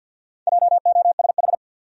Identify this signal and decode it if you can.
Morse code, heard as a keyed tone, about 53 wpm, 700 Hz. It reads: JOSH
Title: USA Common Names - 50wpm